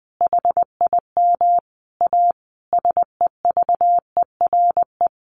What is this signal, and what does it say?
Morse code, heard as a keyed tone, about 20 wpm, 700 Hz. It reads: HIM A SEVERE
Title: Sentences from Top 1400 Words 20wpm